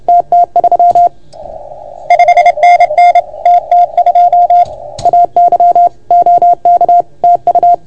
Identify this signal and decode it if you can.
Morse code, heard as a keyed tone, about 30 wpm, 690 Hz. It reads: TT3 5NN TT2 AY OK TU
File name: Archiv / CUC381 / OK1FGD CUC 381